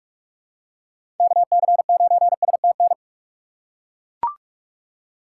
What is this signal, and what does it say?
Morse code, heard as a keyed tone, about 42 wpm, 700 Hz. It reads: KC9STN
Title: USA Call Signs - 6 Characters - 2x Character Spacing - 45wpm